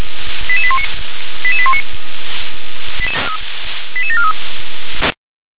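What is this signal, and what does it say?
Five-tone selective call (ZVEI), heard as quick runs of five tones, tones of about 70 ms. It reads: 88819 88819 88743
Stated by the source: ZVEI five-tone selective call received by SDR.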